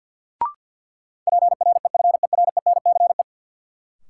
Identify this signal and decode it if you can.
Morse code, heard as a keyed tone, about 49 wpm, 700 Hz. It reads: PREFERENCE
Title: Sentences from Top 1500 Words - New Word Drill - 50wpm